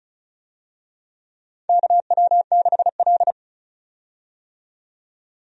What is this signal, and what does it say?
Morse code, heard as a keyed tone, about 34 wpm, 700 Hz. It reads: KW6L
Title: USA Call Signs - 4 Characters - 35wpm